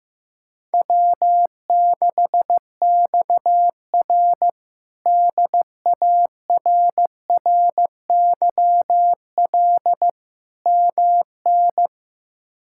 Morse code, heard as a keyed tone, about 15 wpm, 700 Hz. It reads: W6XR DARRYL MN
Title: K1USN SST QSO Practice - 15wpm